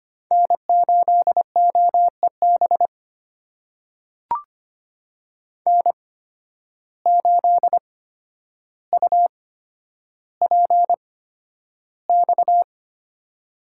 Morse code, heard as a keyed tone, about 25 wpm, spaced as for 11 wpm, 700 Hz. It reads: N8OEB N8UPX
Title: USA Call Signs - 5 Characters - 8x Character Spacing - 25wpm